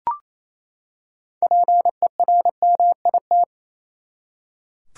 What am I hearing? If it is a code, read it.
Morse code, heard as a keyed tone, about 28 wpm, 700 Hz. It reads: PERMIT
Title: Six Letter Words - 28wpm